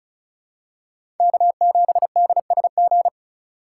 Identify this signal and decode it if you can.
Morse code, heard as a keyed tone, about 36 wpm, 700 Hz. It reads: K7DSG
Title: USA Call Signs - 3x Character Spacing - 35wpm